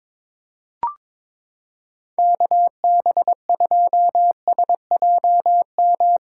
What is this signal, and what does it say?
Morse code, heard as a keyed tone, about 22 wpm, 700 Hz. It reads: KB2SJM
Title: USA Callsign DE Callsign - 6 Characters - 22wpm